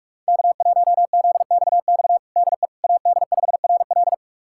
Morse code, heard as a keyed tone, about 46 wpm, 700 Hz. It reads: K1ZXX DE AD5RL
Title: USA Callsign DE Callsign - 5 Characters - 45wpm